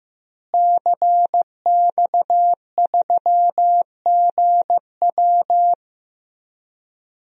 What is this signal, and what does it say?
Morse code, heard as a keyed tone, about 15 wpm, 700 Hz. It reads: CX3GW
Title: DXCC-100 Call Signs - 5 Characters - Repeated 2x - 15wpm